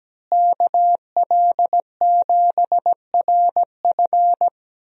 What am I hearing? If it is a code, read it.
Morse code, heard as a keyed tone, about 17 wpm, 700 Hz. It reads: KL7RF
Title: DXCC-100 Call Signs - 5 Characters - Repeated 2x - 17wpm